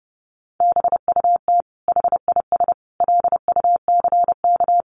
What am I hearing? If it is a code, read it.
Morse code, heard as a keyed tone, about 30 wpm, 700 Hz. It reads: BUT HIS LUCK
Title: Sentences from Top 1500 Words 30wpm